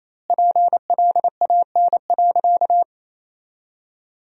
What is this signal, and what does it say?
Morse code, heard as a keyed tone, about 28 wpm, 700 Hz. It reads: PLAN.
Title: 2x Word-Spacing - Sentences from Top 500 Words - 28wpm